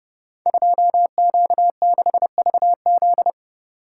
Morse code, heard as a keyed tone, about 30 wpm, 700 Hz. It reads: 2Q6VZ
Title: Code Groups - 5 Letter:Numbers - 30wpm